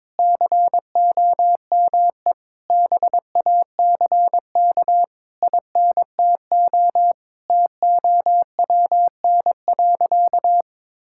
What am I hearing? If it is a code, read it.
Morse code, heard as a keyed tone, about 21 wpm, 700 Hz. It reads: COME BACK INTO TOWN.
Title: Sentences from Top 1000 Words - Review Part 6 - 22wpm